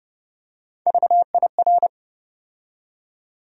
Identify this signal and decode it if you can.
Morse code, heard as a keyed tone, about 30 wpm, 700 Hz. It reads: VIR
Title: USA Call Sign Postfixes - 3 Letters 30wpm